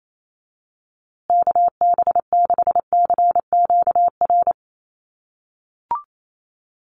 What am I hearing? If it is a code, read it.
Morse code, heard as a keyed tone, about 28 wpm, 700 Hz. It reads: KB6CQR
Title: USA Call Signs Repeated 3x 28wpm